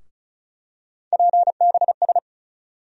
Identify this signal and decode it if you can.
Morse code, heard as a keyed tone, about 36 wpm, 700 Hz. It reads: PBS
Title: USA Call Sign Postfixes - 3 Letters 35wpm